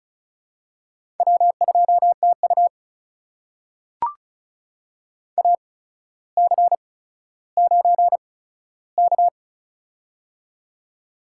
Morse code, heard as a keyed tone, about 36 wpm, spaced as for 15 wpm, 700 Hz. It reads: W2TU AC9K
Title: USA Call Signs - 4 Characters - 8x Character Spacing - 35wpm